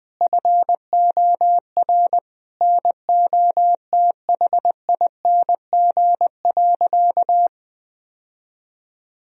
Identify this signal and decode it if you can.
Morse code, heard as a keyed tone, about 20 wpm, 700 Hz. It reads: FOR NOTHING.
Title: Sentences from Top 1000 Words - Review Part 3 - 20wpm